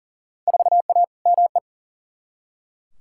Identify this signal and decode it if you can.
Morse code, heard as a keyed tone, about 39 wpm, 700 Hz. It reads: 4A ME
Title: Field Day Exchanges - 40wpm